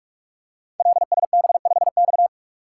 Morse code, heard as a keyed tone, about 44 wpm, 700 Hz. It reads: LSB5X
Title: Code Groups - 5 Letter:Numbers - 45wpm